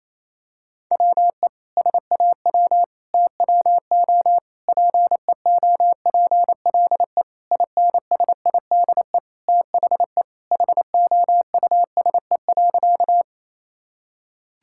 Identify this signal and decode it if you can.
Morse code, heard as a keyed tone, about 28 wpm, 700 Hz. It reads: WE SAW TWO PEOPLE INSIDE THE HOUSE.
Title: Sentences from Top 1000 Words - Review Part 6 - 28wpm